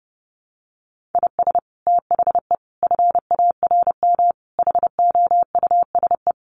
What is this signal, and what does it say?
Morse code, heard as a keyed tone, about 30 wpm, 700 Hz. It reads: IS THE FARM HOUSE
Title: Sentences from Top 1400 Words 30wpm